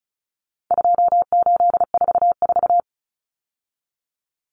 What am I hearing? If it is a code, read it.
Morse code, heard as a keyed tone, about 34 wpm, 700 Hz. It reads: 2844
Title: Numbers - 4-Digit 35wpm